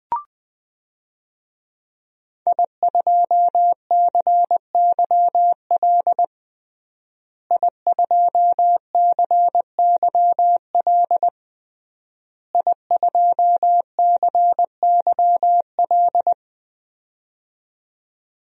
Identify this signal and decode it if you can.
Morse code, heard as a keyed tone, about 21 wpm, 700 Hz. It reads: I2CYL I2CYL I2CYL
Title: DXCC-100 Call Signs - 5 Characters - Repeated 3x - 20wpm